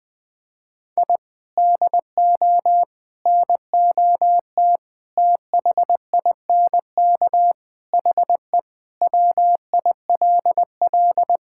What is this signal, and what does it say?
Morse code, heard as a keyed tone, about 21 wpm, 700 Hz. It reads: I DO NOT THINK HE WILL
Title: Sentences with 5-Word Phrases - 20wpm